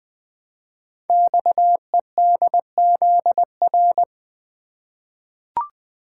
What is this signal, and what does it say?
Morse code, heard as a keyed tone, about 20 wpm, 700 Hz. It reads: XEDZR
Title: Code Groups - 5 Letters - 20wpm